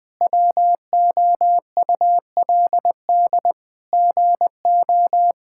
Morse code, heard as a keyed tone, about 20 wpm, 700 Hz. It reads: WOULD GO
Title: Sentences from Top 1600 Words 20wpm